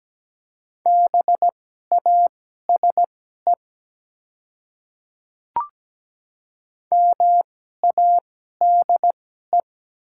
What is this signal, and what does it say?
Morse code, heard as a keyed tone, about 17 wpm, spaced as for 12 wpm, 690 Hz. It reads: BASE MADE
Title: 2x Character Spacing - Top 300 Words - 17wpm